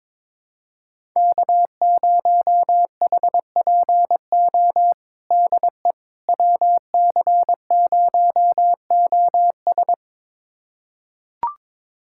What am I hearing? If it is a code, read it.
Morse code, heard as a keyed tone, about 22 wpm, 700 Hz. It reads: K0HPO DE WC0OS
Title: USA Callsign DE Callsign - 5 Characters - 22wpm